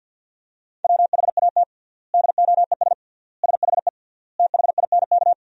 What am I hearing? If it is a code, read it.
Morse code, heard as a keyed tone, about 50 wpm, 700 Hz. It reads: WHAT DOES SHE THINK
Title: 3x Word-Spacing - Sentences from Top 200 Words - 50wpm